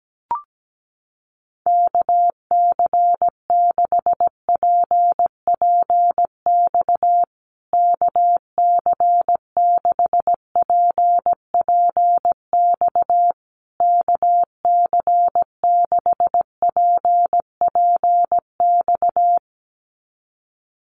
Morse code, heard as a keyed tone, about 17 wpm, 700 Hz. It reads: KC6PPX KC6PPX KC6PPX
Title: USA Call Signs Repeated 3x 17wpm